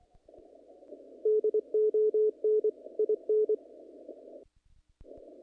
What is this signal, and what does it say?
Morse code, heard as a keyed tone, about 26 wpm, 430 Hz. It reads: DON IN
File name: contests / naqp-cw-k5tr-op-k5pi- -1900